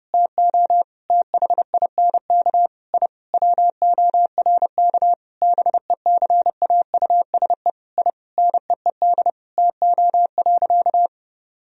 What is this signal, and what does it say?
Morse code, heard as a keyed tone, about 32 wpm, 700 Hz. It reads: TO THINK I WORK BECAUSE I NEED TO.
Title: Sentences from Top 1000 Words - Review Part 3 - 30wpm